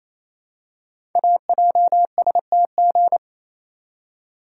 Morse code, heard as a keyed tone, about 28 wpm, 700 Hz. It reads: AJSTG
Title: Code Groups - 5 Letters - 28wpm